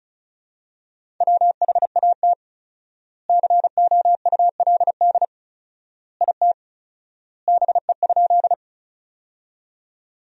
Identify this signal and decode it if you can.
Morse code, heard as a keyed tone, about 36 wpm, 700 Hz. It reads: WHAT COULD IT BE?
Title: 4x Word-Spacing - Sentences from Top 100 Words - 35wpm